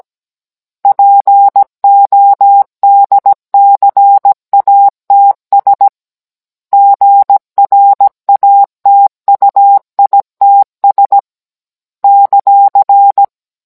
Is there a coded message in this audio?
Yes: Morse code from 0.8 to 13.2 s, heard as a keyed tone, about 17 wpm, 800 Hz. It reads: PODCATS GRATUITS ;